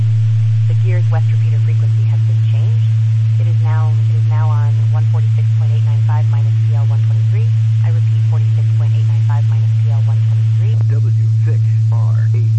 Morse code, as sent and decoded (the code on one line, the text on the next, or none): none